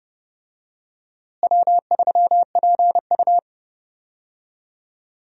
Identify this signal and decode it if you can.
Morse code, heard as a keyed tone, about 30 wpm, 700 Hz. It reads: W3PU
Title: USA Call Signs - 4 Characters - Repeated 2x - 30wpm